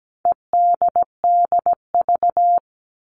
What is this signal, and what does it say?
Morse code, heard as a keyed tone, about 17 wpm, 700 Hz. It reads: EDDV
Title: All Cut Numbers - 4-Digit 17wpm